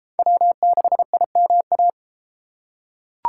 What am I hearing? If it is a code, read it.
Morse code, heard as a keyed tone, about 33 wpm, 700 Hz. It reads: W6IMA